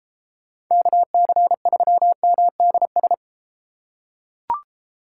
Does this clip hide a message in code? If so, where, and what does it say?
Yes: Morse code from 0.7 to 3.1 s, heard as a keyed tone, about 31 wpm, 700 Hz. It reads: KC3MDS